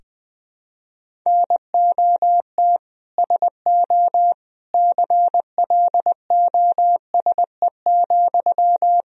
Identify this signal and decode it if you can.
Morse code, heard as a keyed tone, about 21 wpm, 700 Hz. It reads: NOT SO CLOSE,
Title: Sentences from Top 1000 Words - Review Part 1 - 20wpm